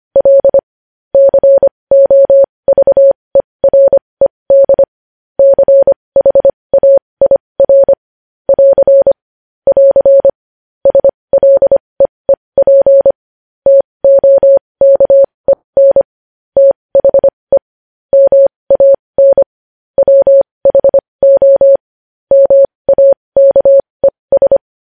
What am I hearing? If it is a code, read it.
Morse code, heard as a keyed tone, about 25 wpm, spaced as for 20 wpm, 550 Hz. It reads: L COVERED CHAIR <AR> <AR> SLEEP TOKEN THE MAN WHO MAKES
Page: news headlines in morse code at 20 wpm